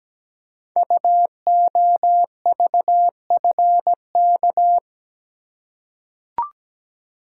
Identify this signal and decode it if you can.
Morse code, heard as a keyed tone, about 17 wpm, 700 Hz. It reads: UOVFK
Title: Code Groups - 5 Letters - 17wpm